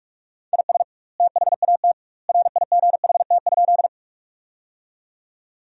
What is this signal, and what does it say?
Morse code, heard as a keyed tone, about 43 wpm, 700 Hz. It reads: IS THAT RIGHT?
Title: 2x Word-Spacing - Sentences from Top 200 Words - 45wpm